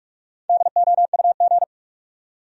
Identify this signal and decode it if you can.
Morse code, heard as a keyed tone, about 45 wpm, 700 Hz. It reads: DOUG
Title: USA Common Names - Max 4 Letters - 45wpm